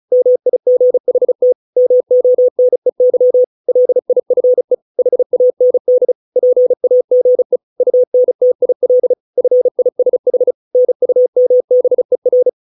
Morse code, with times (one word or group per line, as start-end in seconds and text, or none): 0.1-1.5 MIGHT
1.8-3.4 MONEY
3.7-4.7 LIFE
5.0-6.1 HAND
6.4-7.6 PAGE
7.8-9.1 UNTIL
9.4-10.5 FISH
10.7-12.5 NUMBER